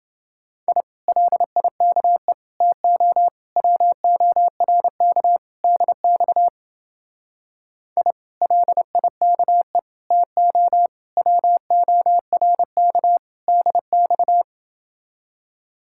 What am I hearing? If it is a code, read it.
Morse code, heard as a keyed tone, about 28 wpm, 700 Hz. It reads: I LIKE TO WORK DX I LIKE TO WORK DX